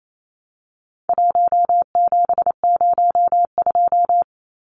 Morse code, heard as a keyed tone, about 28 wpm, 700 Hz. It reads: 1702
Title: Numbers - 4-Digit 28wpm